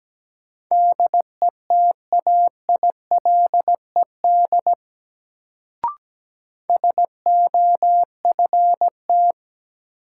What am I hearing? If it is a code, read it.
Morse code, heard as a keyed tone, about 17 wpm, 700 Hz. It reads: DETAILED SOFT
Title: Sentences from Top 1500 Words - New Word Drill Alt - 17wpm